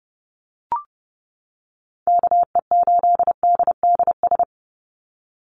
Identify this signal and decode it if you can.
Morse code, heard as a keyed tone, about 32 wpm, 700 Hz. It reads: KE8DDS
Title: USA Call Signs 30wpm